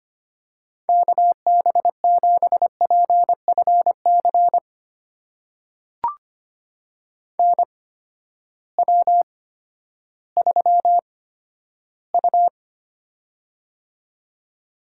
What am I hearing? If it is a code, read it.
Morse code, heard as a keyed tone, about 25 wpm, spaced as for 12 wpm, 700 Hz. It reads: KB7PFC NW3U
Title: USA Call Signs - 8x Character Spacing - 25wpm